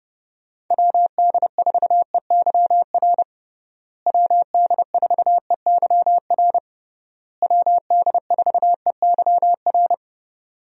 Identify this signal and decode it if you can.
Morse code, heard as a keyed tone, about 29 wpm, 700 Hz. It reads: WD4EYR WD4EYR WD4EYR
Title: USA Call Signs - 6 Characters - Repeated 3x - 30wpm